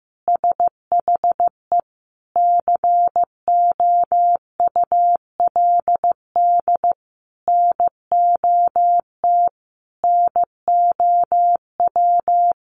Morse code, heard as a keyed tone, about 15 wpm, 700 Hz. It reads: SHE COULD NOT NOW
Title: Sentences from Top 1200 Words 15wpm